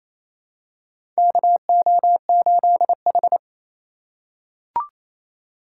Morse code, heard as a keyed tone, about 28 wpm, 700 Hz. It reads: KO8H
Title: USA Call Signs - 4 Characters - 4x Character Spacing - 28wpm